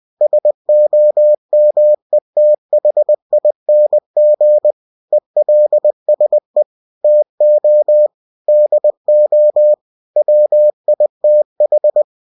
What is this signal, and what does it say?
Morse code, heard as a keyed tone, about 20 wpm, 600 Hz. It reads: SOMETHING ELSE TO DO WITH
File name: cw-books / winnie-the-pooh / winnie-20wpm-page-0093